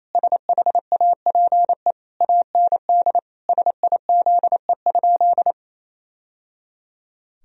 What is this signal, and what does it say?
Morse code, heard as a keyed tone, about 27 wpm, 700 Hz. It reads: SHAPE AND SIZE?
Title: Sentences from Top 1000 Words - Review Part 3 - 28wpm